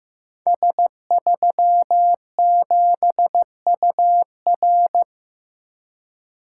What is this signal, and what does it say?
Morse code, heard as a keyed tone, about 15 wpm, 700 Hz. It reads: S37UR
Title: Code Groups - 5 Letter:Numbers - 15wpm